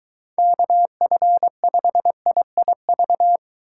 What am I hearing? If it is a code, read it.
Morse code, heard as a keyed tone, about 23 wpm, 700 Hz. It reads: KF5IIV